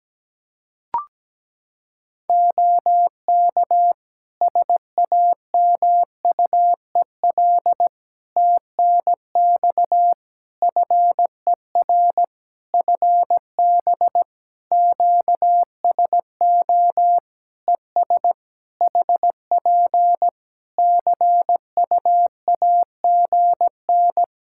Morse code, heard as a keyed tone, about 17 wpm, 700 Hz. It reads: OK SAMUEL TNX FER FB QSO ES HP CUAGN
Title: K1USN SST QSO Practice 17wpm